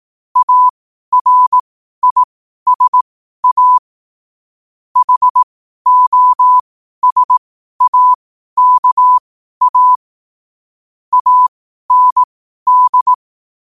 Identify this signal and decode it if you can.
Morse code, heard as a keyed tone, about 18 wpm, spaced as for 12 wpm, 1000 Hz. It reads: ARISA HOSAKA AND
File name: qotdfiles / 12wpm